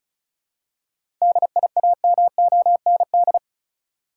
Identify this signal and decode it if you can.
Morse code, heard as a keyed tone, about 34 wpm, 700 Hz. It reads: DIAMOND
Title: Seven Letter Words - 35wpm